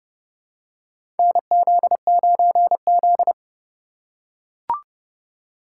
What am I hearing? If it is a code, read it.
Morse code, heard as a keyed tone, about 31 wpm, 700 Hz. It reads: NZ9Z
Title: USA Call Signs - 4 Characters - 3x Character Spacing - 30wpm